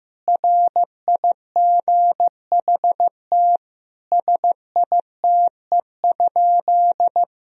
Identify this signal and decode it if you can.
Morse code, heard as a keyed tone, about 15 wpm, 700 Hz. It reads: RIGHT SITE?
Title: Sentences from Top 1000 Words - Review Part 4 - 15wpm